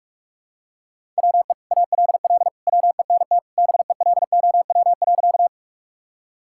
Morse code, heard as a keyed tone, about 46 wpm, 700 Hz. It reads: WE ALL WENT BELOW.
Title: Sentences from Top 1000 Words - Review Part 5 - 45wpm